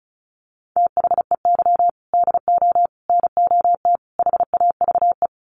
Morse code, heard as a keyed tone, about 34 wpm, 700 Hz. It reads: THEY DO NOT HAVE
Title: Sets of 4 Words 35wpm